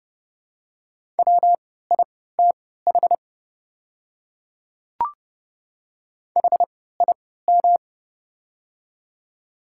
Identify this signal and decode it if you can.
Morse code, heard as a keyed tone, about 30 wpm, spaced as for 17 wpm, 690 Hz. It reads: WITH HIM
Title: 3x Character Spacing - Top 300 Words - 30wpm